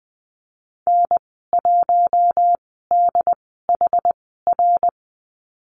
Morse code, heard as a keyed tone, about 20 wpm, spaced as for 15 wpm, 700 Hz. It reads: N1DHR
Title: 167 Lesson 27g - Number 7 - Call Signs 20wpm